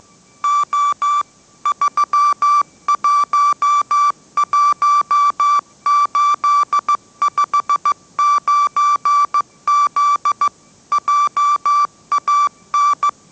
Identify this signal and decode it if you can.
Morse code, heard as a keyed tone, about 15 wpm, 1160 Hz. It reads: O 311859Z JAN
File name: mrx / mrx065 16wpm